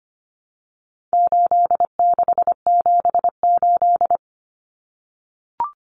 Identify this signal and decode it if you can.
Morse code, heard as a keyed tone, about 25 wpm, 700 Hz. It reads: 8678
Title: Numbers - 4-Digit 25wpm